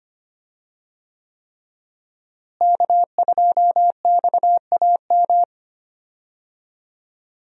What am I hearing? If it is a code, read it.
Morse code, heard as a keyed tone, about 25 wpm, 700 Hz. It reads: K2XAM
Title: USA Call Signs - 5 Characters - Repeated 3x - 25wpm